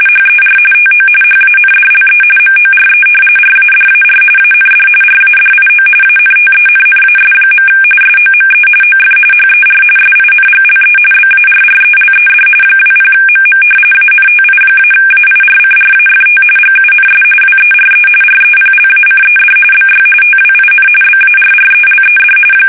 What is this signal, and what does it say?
RTTY, about 45.45 baud. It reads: HER TO HANDLE MORE RADAR AND RADIO ANTENNAS. SOME OF HER SINGLE BARRELED 20MM ANTI-AIRCRAFT GUNS WERE REPLACED BY TWIN BARRELED
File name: fsk850 / battleshipnj